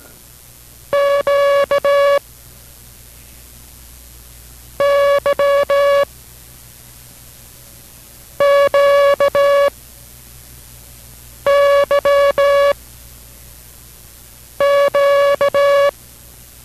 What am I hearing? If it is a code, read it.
Morse code, heard as a keyed tone, about 17 wpm, 560 Hz. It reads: Q Y Q Y Q